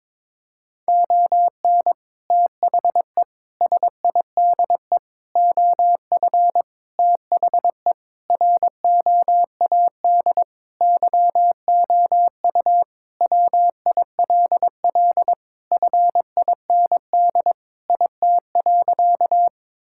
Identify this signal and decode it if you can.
Morse code, heard as a keyed tone, about 22 wpm, 700 Hz. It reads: ON THE SIDE OF THE ROAD YOU WILL FIND IT.
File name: practice / 22wpm / Sentences with 3-Word Phrases - 22wpm